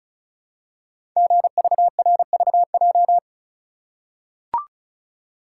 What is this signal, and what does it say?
Morse code, heard as a keyed tone, about 36 wpm, 700 Hz. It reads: GVRVJ
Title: Code Groups - 5 Letter:Numbers - 35wpm